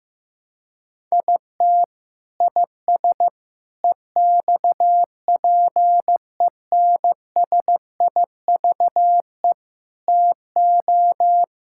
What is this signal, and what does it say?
Morse code, heard as a keyed tone, about 16 wpm, 700 Hz. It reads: IT IS EXPENSIVE TO
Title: Sentences from Top 1000 Words - Review Part 6 - 15wpm